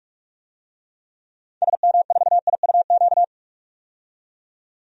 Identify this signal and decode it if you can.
Morse code, heard as a keyed tone, about 46 wpm, 700 Hz. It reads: SM4IUQ
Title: DXCC-100 Call Signs - 6 Characters - 45wpm